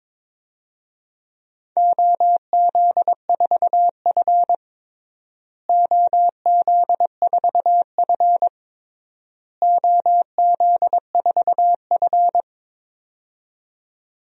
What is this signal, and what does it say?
Morse code, heard as a keyed tone, about 22 wpm, 700 Hz. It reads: OZ4F OZ4F OZ4F
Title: DXCC-100 Call Signs - 4 Characters - Repeated 3x - 22wpm